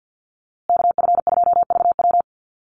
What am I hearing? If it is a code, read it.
Morse code, heard as a keyed tone, about 49 wpm, 700 Hz. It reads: KF2VW
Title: USA Call Signs Repeated 3x 50wpm